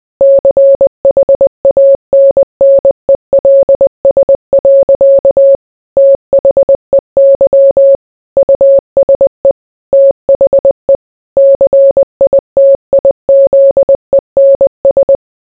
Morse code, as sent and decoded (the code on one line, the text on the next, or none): -.-. .... .- -. -. .-.. ... .-.-.- / - .... -.-- / ..- ... / - .... / -.-. .. - .. --.. -. ...
CHANNELS. THEY USE THE CITIZENS